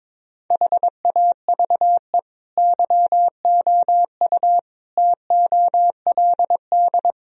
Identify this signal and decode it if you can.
Morse code, heard as a keyed tone, about 22 wpm, 700 Hz. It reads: HAVE YOU TOLD
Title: Sentences from Top 1000 Words - Review Part 2 - 22wpm